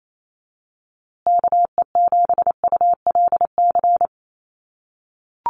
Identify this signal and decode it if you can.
Morse code, heard as a keyed tone, about 28 wpm, 700 Hz. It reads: KE7ULC
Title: USA Call Signs Repeated 2x 28wpm